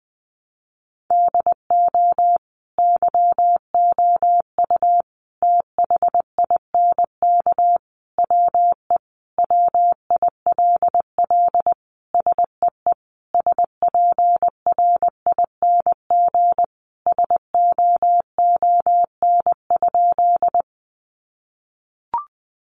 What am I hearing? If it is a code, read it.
Morse code, heard as a keyed tone, about 19 wpm, 700 Hz. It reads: DO YOU THINK WE WILL SEE SPRING SOON?
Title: Sentences from Top 1200 Words 20wpm